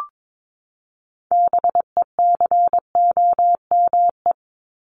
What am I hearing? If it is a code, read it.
Morse code, heard as a keyed tone, about 22 wpm, 700 Hz. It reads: BECOME
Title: Top 501-750 Words 22wpm